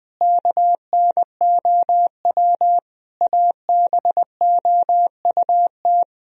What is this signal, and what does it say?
Morse code, heard as a keyed tone, about 21 wpm, 700 Hz. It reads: KNOW ABOUT
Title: Sentences from Top 1000 Words - Review Part 4 - 20wpm